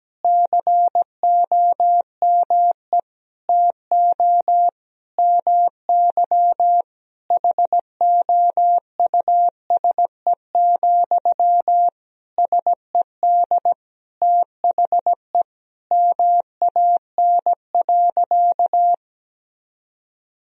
Morse code, heard as a keyed tone, about 17 wpm, 700 Hz. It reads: COME TO MY HOUSE, SED THE MAN.
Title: Sentences from Top 400 Words and Abbreviations 17wpm